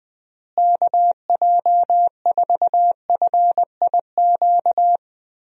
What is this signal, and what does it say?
Morse code, heard as a keyed tone, about 20 wpm, 700 Hz. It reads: KJ4FIQ
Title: USA Call Signs - 6 Characters - Repeated 3x - 20wpm